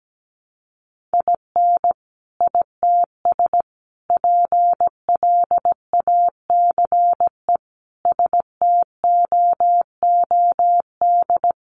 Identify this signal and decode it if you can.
Morse code, heard as a keyed tone, about 17 wpm, 700 Hz. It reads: IN ITS PLACE STOOD
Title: Sentences from Top 1400 Words 17wpm